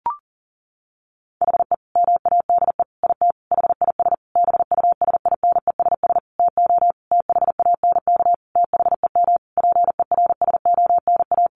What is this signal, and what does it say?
Morse code, heard as a keyed tone, about 40 wpm, 700 Hz. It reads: HE MADE IT HIS BUSINESS TO THANK THEM PERSONA
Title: Sentences from Top 1400 Words 40wpm